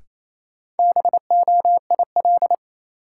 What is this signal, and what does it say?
Morse code, heard as a keyed tone, about 28 wpm, 700 Hz. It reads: BOIL
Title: Sets of 5 Words Encore - New Word Drill - 28wpm